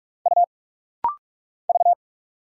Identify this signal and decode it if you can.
Morse code, heard as a keyed tone, about 46 wpm, 700 Hz. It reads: U V
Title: Single Letters - Rapid-Fire 45wpm